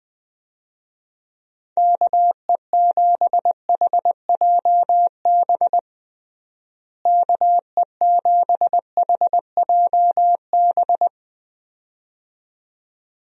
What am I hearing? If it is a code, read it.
Morse code, heard as a keyed tone, about 21 wpm, 700 Hz. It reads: KE7HJB KE7HJB
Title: USA Call Signs - 6 Characters - Repeated 2x - 20wpm